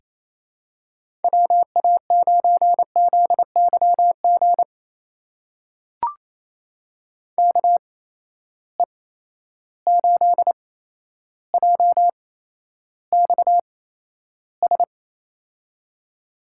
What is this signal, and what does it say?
Morse code, heard as a keyed tone, about 27 wpm, spaced as for 12 wpm, 700 Hz. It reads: WA9ZYG KE8JXS
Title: USA Call Signs - 8x Character Spacing - 28wpm